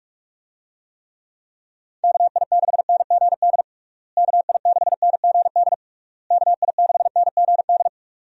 Morse code, heard as a keyed tone, about 46 wpm, 700 Hz. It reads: KI6NGD KI6NGD KI6NGD
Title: USA Call Signs - 6 Characters - Repeated 3x - 45wpm